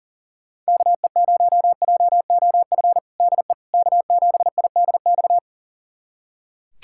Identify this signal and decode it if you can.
Morse code, heard as a keyed tone, about 40 wpm, 700 Hz. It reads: KE0JOF DE K7IDX
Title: USA Callsign DE Callsign 40wpm